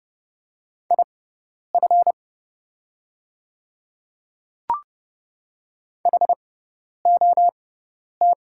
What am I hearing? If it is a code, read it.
Morse code, heard as a keyed tone, about 30 wpm, spaced as for 10 wpm, 700 Hz. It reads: IF HOT
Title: 6x Character Spacing - Top 300 Words - 30wpm